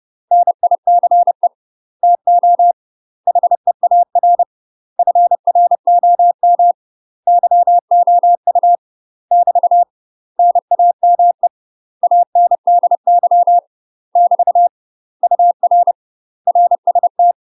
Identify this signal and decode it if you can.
Morse code, heard as a keyed tone, about 30 wpm, 700 Hz. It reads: NICE TO HEAR FROM YOU <BT> NAME ANDY <BT> UR RST
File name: UK MASTER QSOs TRAINING / 30WPM / UK MASTER QSOs 0028